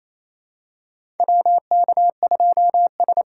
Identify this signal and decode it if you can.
Morse code, heard as a keyed tone, about 28 wpm, 700 Hz. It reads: WK2S
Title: USA Callsign DE Callsign - 4 Characters - 28wpm